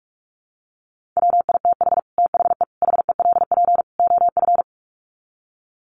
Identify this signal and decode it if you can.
Morse code, heard as a keyed tone, about 44 wpm, 700 Hz. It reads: WITH THE HELP OF
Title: Sets of 4 Words 45wpm